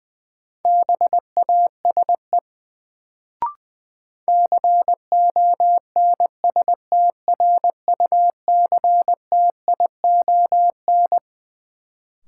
Morse code, heard as a keyed tone, about 20 wpm, 700 Hz. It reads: BASE CONSTRUCTION
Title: Sentences from Top 1600 Words - New Word Drill Alt - 20wpm